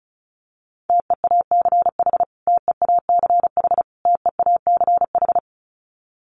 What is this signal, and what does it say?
Morse code, heard as a keyed tone, about 34 wpm, 700 Hz. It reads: TEACH TEACH TEACH
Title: Top 500 Words Repeated 35wpm